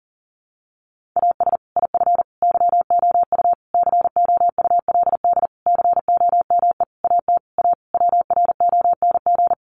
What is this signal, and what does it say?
Morse code, heard as a keyed tone, about 40 wpm, 700 Hz. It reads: AS IF YOU COULD COME AT A WRONG
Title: Sentences from Top 900 Words 40wpm